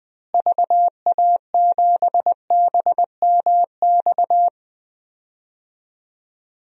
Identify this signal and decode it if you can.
Morse code, heard as a keyed tone, about 20 wpm, 700 Hz. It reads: VA7BMX
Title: DXCC-100 Call Signs - 6 Characters - Repeated 3x - 20wpm